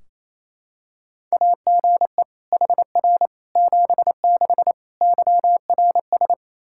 Morse code, heard as a keyed tone, about 28 wpm, 700 Hz. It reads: AGE HR 76 YRS
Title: Practice QSO - Exchange 3 - More Info - 28wpm